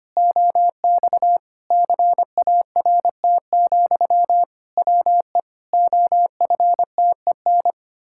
Morse code, heard as a keyed tone, about 25 wpm, 700 Hz. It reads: OX CART, WE OFTEN
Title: Pangram Challenge v2 - 25wpm